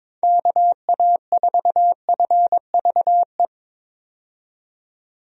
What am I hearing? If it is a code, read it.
Morse code, heard as a keyed tone, about 22 wpm, 700 Hz. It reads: KA4FVE
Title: USA Call Signs - 6 Characters - 22wpm